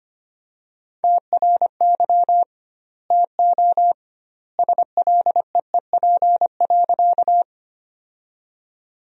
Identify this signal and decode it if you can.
Morse code, heard as a keyed tone, about 25 wpm, 700 Hz. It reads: TRY TO SLEEP.
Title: 2x Word-Spacing - Sentences from Top 500 Words - 25wpm